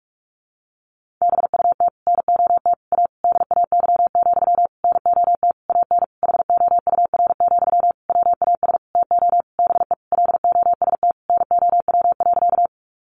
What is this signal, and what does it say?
Morse code, heard as a keyed tone, about 44 wpm, 700 Hz. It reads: BUT NOT A DAY, NOT AN HOUR, WAS TO BE LOST NOW.
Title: Sentences from Top 1400 Words 45wpm